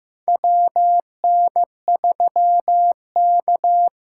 Morse code, heard as a keyed tone, about 16 wpm, 700 Hz. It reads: WN3K
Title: K1USN SST QSO Practice - 15wpm